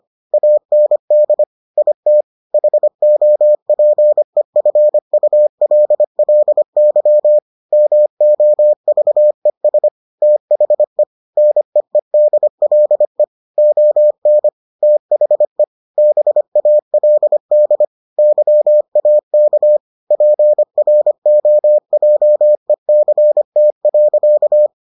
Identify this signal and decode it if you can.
Morse code, heard as a keyed tone, about 25 wpm, 600 Hz. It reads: AND IT HOPEFULLY MOVES THE NEEDLE ON THE BALD YAK PROJECT.